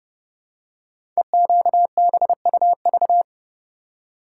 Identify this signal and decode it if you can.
Morse code, heard as a keyed tone, about 30 wpm, 700 Hz. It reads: EQBUV
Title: Code Groups - 5 Letters - 30wpm